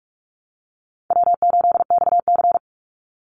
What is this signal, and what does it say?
Morse code, heard as a keyed tone, about 44 wpm, 700 Hz. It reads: W8XC
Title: Easy Call Signs Repeated 2x - 45wpm